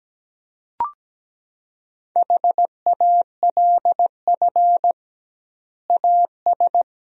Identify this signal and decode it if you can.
Morse code, heard as a keyed tone, about 17 wpm, 700 Hz. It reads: HALF AS
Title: 2x Word-Spacing - Sentences from Top 400 Words - 17wpm